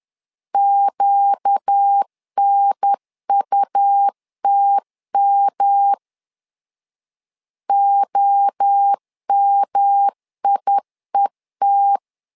Morse code, heard as a keyed tone, about 10 wpm, 800 Hz. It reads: QNUTM OMIET
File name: LEZIONI / 4 Q 10wpm 2210